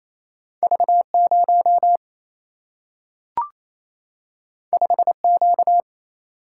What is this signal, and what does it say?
Morse code, heard as a keyed tone, about 28 wpm, 700 Hz. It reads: V0 5Q
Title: Single-Letter Single-Number 28wpm